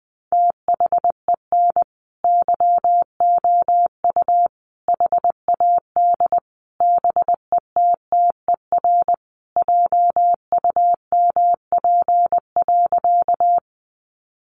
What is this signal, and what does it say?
Morse code, heard as a keyed tone, about 20 wpm, 700 Hz. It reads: THEN YOU HAD BETTER JUMP.
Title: Sentences from Top 1300 Words 20wpm